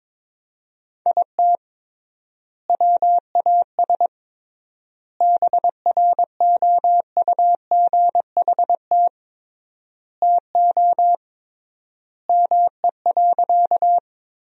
Morse code, heard as a keyed tone, about 22 wpm, 700 Hz. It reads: IT WAS BROUGHT TO ME.
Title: 3x Word-Spacing - Sentences from Top 500 Words - 22wpm